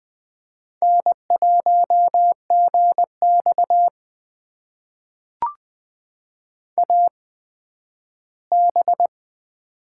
Morse code, heard as a keyed tone, about 20 wpm, spaced as for 12 wpm, 700 Hz. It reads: N1GX AB
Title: USA Call Signs - 4 Characters - 8x Character Spacing - 20wpm